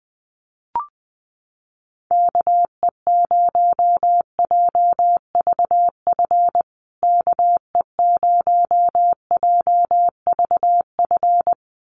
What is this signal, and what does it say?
Morse code, heard as a keyed tone, about 19 wpm, 700 Hz. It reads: KE0JVF KE0JVF
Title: USA Call Signs Repeated 2x 20wpm